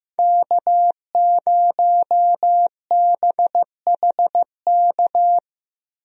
Morse code, heard as a keyed tone, about 15 wpm, 700 Hz. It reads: K0BHK
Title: USA Call Signs - 5 Characters - Repeated 3x - 15wpm